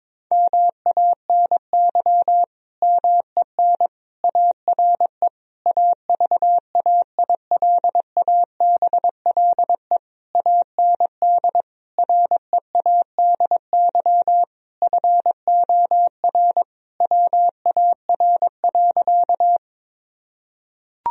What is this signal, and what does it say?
Morse code, heard as a keyed tone, about 22 wpm, 700 Hz. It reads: MANY MEN ARE AVAILABLE AND READY FOR WAR.
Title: Sentences from Top 1000 Words - Review Part 3 - 22wpm